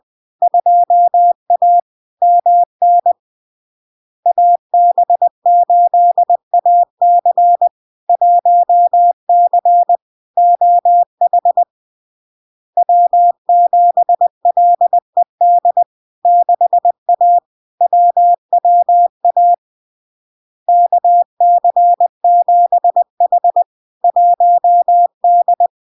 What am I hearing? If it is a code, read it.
Morse code, heard as a keyed tone, about 20 wpm, 700 Hz. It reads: 2A MN AB8AC 1C OH W7LED 6A WWA KC7H 1D
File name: LARK-FD-2022-Log-in-CW-20wpm-700-Hz